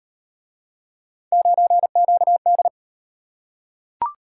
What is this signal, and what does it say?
Morse code, heard as a keyed tone, about 36 wpm, 700 Hz. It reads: 9QD